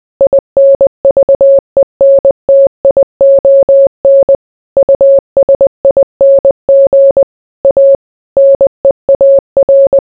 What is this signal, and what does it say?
Morse code, heard as a keyed tone, about 20 wpm, 550 Hz. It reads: INVENTION USING A NEAR